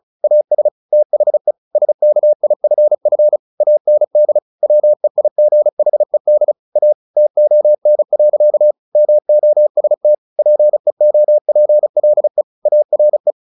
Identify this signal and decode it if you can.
Morse code, heard as a keyed tone, about 34 wpm, 600 Hz. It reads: AS THE SKIFF AND WEIGHED A TON. MOST PEOPLE ARE
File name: the old man and the sea / the old man and the sea 1 35wpm